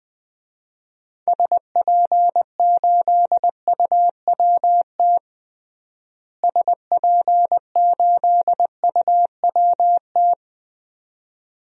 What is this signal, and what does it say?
Morse code, heard as a keyed tone, about 21 wpm, 700 Hz. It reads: SP8UWT SP8UWT
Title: DXCC-100 Call Signs - 6 Characters - Repeated 2x - 20wpm